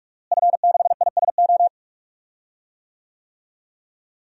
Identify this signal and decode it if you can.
Morse code, heard as a keyed tone, about 48 wpm, 700 Hz. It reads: F6ISO
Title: DXCC-100 Call Signs - 5 Characters - Repeated 3x - 45wpm